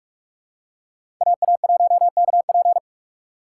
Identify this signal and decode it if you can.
Morse code, heard as a keyed tone, about 43 wpm, 700 Hz. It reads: AA1KP
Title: USA Call Signs - 5 Characters - 2x Character Spacing - 45wpm